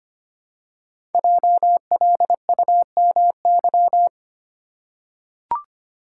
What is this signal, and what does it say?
Morse code, heard as a keyed tone, about 25 wpm, 700 Hz. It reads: JLUMY
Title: Code Groups - 5 Letters - 25wpm